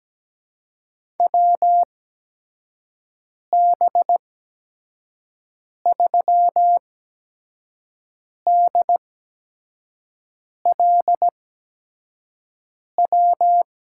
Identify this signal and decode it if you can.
Morse code, heard as a keyed tone, about 17 wpm, spaced as for 5 wpm, 700 Hz. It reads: WB3DLW
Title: USA Call Signs - 6 Characters - 8x Character Spacing - 17wpm